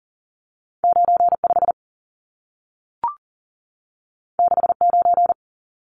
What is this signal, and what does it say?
Morse code, heard as a keyed tone, about 40 wpm, 700 Hz. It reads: 95 69
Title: Numbers - 2-Digit 40wpm